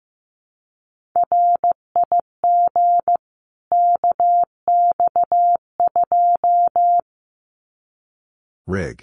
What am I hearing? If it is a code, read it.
Morse code, heard as a keyed tone, about 15 wpm, 700 Hz. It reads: RIG KX2
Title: Common QSO Phrases 15wpm